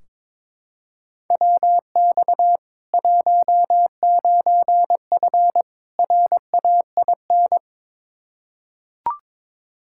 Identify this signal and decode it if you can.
Morse code, heard as a keyed tone, about 22 wpm, 700 Hz. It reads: WX 19F RAIN
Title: K1USN SST QSO Practice 22wpm